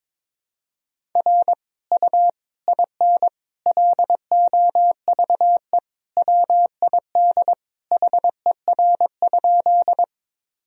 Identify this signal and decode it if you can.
Morse code, heard as a keyed tone, about 22 wpm, 700 Hz. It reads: R U IN LOVE WID HER?
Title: Sentences from Top 400 Words and Abbreviations 22wpm